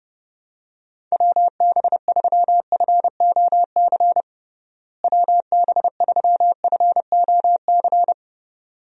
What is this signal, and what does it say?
Morse code, heard as a keyed tone, about 31 wpm, 700 Hz. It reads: WB3FOC WB3FOC
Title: USA Call Signs - 6 Characters - Repeated 2x - 30wpm